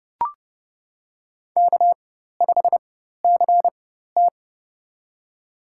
Morse code, heard as a keyed tone, about 32 wpm, spaced as for 14 wpm, 700 Hz. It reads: K5CT
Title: USA Call Signs - 4 Characters - 4x Character Spacing - 30wpm